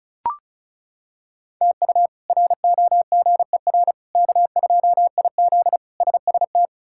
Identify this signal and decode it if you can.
Morse code, heard as a keyed tone, about 34 wpm, 700 Hz. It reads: TU ROGER K2IZ SST
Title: K1USN SST QSO Practice - 35wpm